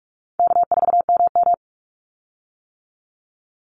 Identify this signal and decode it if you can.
Morse code, heard as a keyed tone, about 44 wpm, 700 Hz. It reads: K4MM
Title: Call Signs - 45wpm